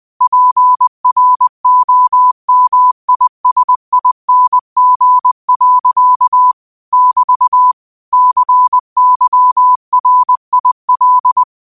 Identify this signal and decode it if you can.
Morse code, heard as a keyed tone, about 20 wpm, 1000 Hz. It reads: PROMISING. <BT> CYRIL
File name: qotdfiles / 20wpm